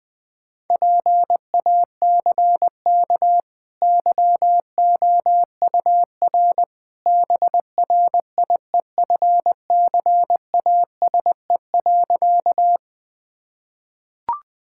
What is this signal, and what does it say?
Morse code, heard as a keyed tone, about 20 wpm, 700 Hz. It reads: PACK YOUR BRIEFCASE.
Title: Easy Sentences with Hints - 20wpm